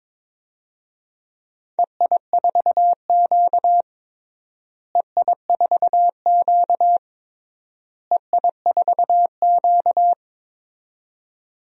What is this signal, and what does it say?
Morse code, heard as a keyed tone, about 22 wpm, 700 Hz. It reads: EI4Q EI4Q EI4Q
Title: DXCC-100 Call Signs - 4 Characters - Repeated 3x - 22wpm